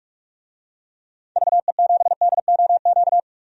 Morse code, heard as a keyed tone, about 46 wpm, 700 Hz. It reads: VE7DOQ
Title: DXCC-100 Call Signs - 6 Characters - 45wpm